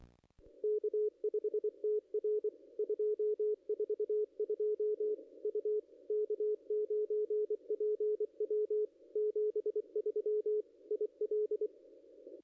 Morse code, heard as a keyed tone, about 26 wpm, 440 Hz, with radio noise. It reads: K5TR 242 U K9PW 73 IL
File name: contests / ss-cw-k5tr-op-n5rz -0400